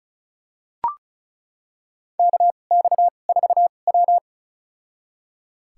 Morse code, heard as a keyed tone, about 34 wpm, spaced as for 25 wpm, 700 Hz. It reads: KX4W
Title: USA Call Signs - 4 Characters - 2x Character Spacing - 35wpm